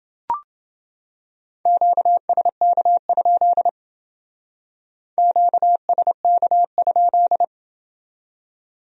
Morse code, heard as a keyed tone, about 29 wpm, 700 Hz. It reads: QSK? QSK?